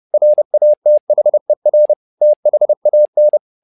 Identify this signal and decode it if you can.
Morse code, heard as a keyed tone, about 30 wpm, 600 Hz. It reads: RATHER THAN
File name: learning center / tower trouble / tower trouble 30wpm